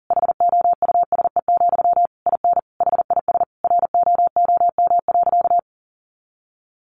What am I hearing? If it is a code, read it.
Morse code, heard as a keyed tone, about 37 wpm, 700 Hz. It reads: HOUSE, IN HIS ROOM.
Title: Sentences from Top 300 Words 40wpm